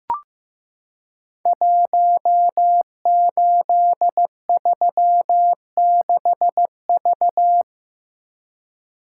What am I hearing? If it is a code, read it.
Morse code, heard as a keyed tone, about 15 wpm, 700 Hz. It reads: 1836V
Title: Code Groups - 5 Letter:Numbers - 15wpm